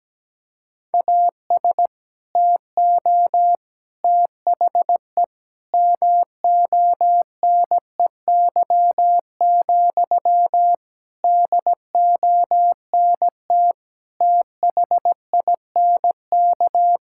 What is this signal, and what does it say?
Morse code, heard as a keyed tone, about 17 wpm, 700 Hz. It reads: AS TO THE MONEY, DONT THINK
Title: Sentences from Top 1000 Words - Review Part 5 - 17wpm